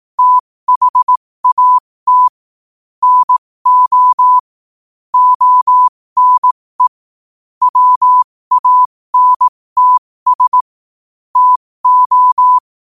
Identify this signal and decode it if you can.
Morse code, heard as a keyed tone, about 18 wpm, spaced as for 15 wpm, 1000 Hz. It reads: THAT NO ONE WANTS TO
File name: 15wpm